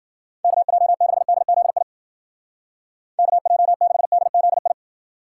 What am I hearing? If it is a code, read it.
Morse code, heard as a keyed tone, about 52 wpm, 700 Hz. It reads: KJ6DZI KJ6DZI